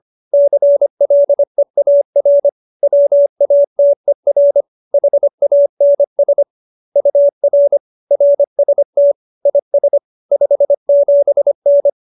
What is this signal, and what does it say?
Morse code, heard as a keyed tone, about 25 wpm, 570 Hz. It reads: CLEAR WATER HANS UR RST IS 57N